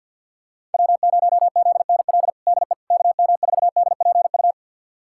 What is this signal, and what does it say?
Morse code, heard as a keyed tone, about 53 wpm, 700 Hz. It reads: W0ZNL DE KM4DPU
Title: USA Callsign DE Callsign 50wpm